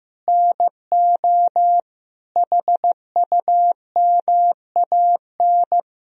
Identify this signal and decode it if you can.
Morse code, heard as a keyed tone, about 15 wpm, 700 Hz. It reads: NO HUMAN
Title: Sentences from Top 1000 Words - Review Part 3 - 15wpm